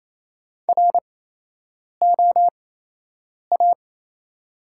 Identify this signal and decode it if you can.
Morse code, heard as a keyed tone, about 28 wpm, spaced as for 8 wpm, 700 Hz. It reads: ROA
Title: 8x Character Spacing - Top 500 Words - 28wpm